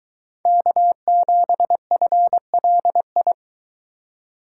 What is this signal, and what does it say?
Morse code, heard as a keyed tone, about 23 wpm, 700 Hz. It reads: K7FLI